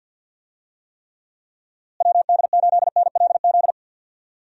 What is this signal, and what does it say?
Morse code, heard as a keyed tone, about 50 wpm, 700 Hz. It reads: WD8NLZ
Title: USA Call Signs - 6 Characters - Repeated 2x - 50wpm